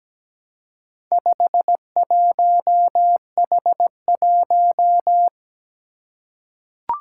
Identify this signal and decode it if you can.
Morse code, heard as a keyed tone, about 17 wpm, 700 Hz. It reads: 51H1
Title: Four Challenging Letter-Number Combinations 17wpm